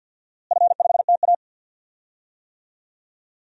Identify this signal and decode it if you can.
Morse code, heard as a keyed tone, about 46 wpm, 700 Hz. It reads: F5TA
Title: DXCC-100 Call Signs - 4 Characters - Repeated 3x - 50wpm